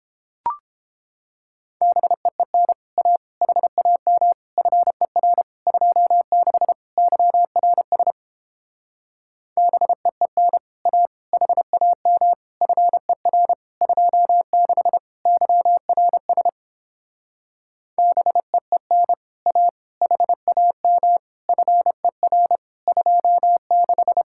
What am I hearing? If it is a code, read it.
Morse code, heard as a keyed tone, about 31 wpm, 700 Hz. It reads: BEEN A HAM FER 26 YRS BEEN A HAM FER 26 YRS BEEN A HAM FER 26